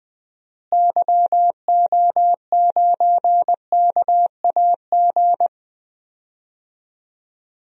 Morse code, heard as a keyed tone, about 20 wpm, 700 Hz. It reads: YO9KAG
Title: DXCC-100 Call Signs - 6 Characters - Repeated 3x - 20wpm